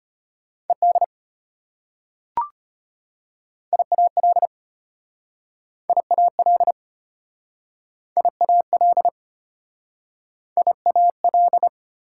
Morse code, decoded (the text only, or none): ED IAL IAL IAL IAL